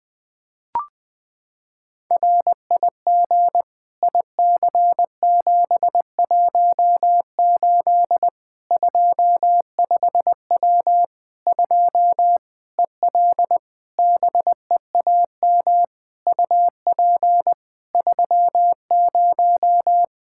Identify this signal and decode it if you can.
Morse code, heard as a keyed tone, about 20 wpm, 700 Hz. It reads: RIG IC718 25W 2 EL BEAM UP 30
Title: Practice QSO - Exchange 2 - Info - 20wpm